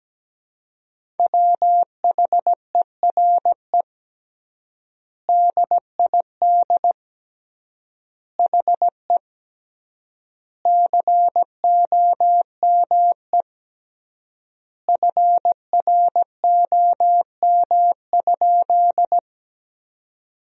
Morse code, heard as a keyed tone, about 17 wpm, 700 Hz. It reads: WHERE DID HE COME FROM?
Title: 3x Word-Spacing - Sentences from Top 200 Words - 17wpm